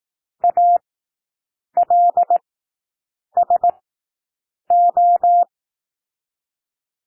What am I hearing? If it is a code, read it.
Morse code, heard as a keyed tone, about 18 wpm, spaced as for 7 wpm, 700 Hz. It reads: ALSO
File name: SOUND WORDS 18 wpm